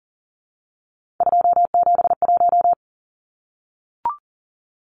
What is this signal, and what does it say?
Morse code, heard as a keyed tone, about 39 wpm, 700 Hz. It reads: 271
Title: Numbers - 3-Digit 40wpm